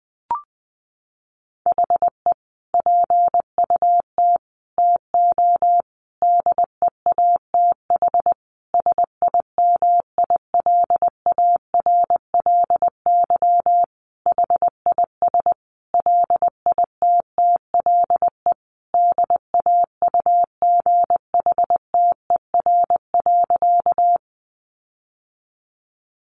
Morse code, heard as a keyed tone, about 20 wpm, 700 Hz. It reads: HE PUT TO DEATH SIMILARLY HIS LITTLE DAUGHTER.
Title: Sentences from Top 1500 Words 20wpm